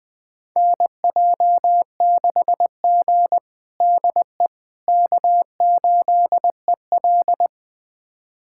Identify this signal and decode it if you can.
Morse code, heard as a keyed tone, about 21 wpm, 700 Hz. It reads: NJ6G DE K8EL
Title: USA Callsign DE Callsign - 4 Characters - 20wpm